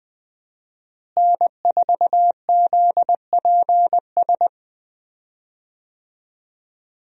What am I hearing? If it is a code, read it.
Morse code, heard as a keyed tone, about 20 wpm, 700 Hz. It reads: N4ZPS
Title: USA Call Signs - 5 Characters - Repeated 3x - 20wpm